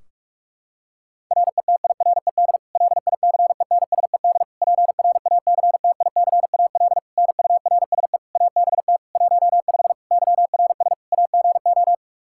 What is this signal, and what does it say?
Morse code, heard as a keyed tone, about 46 wpm, 700 Hz. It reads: RETIRED LICENSED PRACTICAL NURSE ABT 15 YRS AGO
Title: K1USN SST QSO Practice 45wpm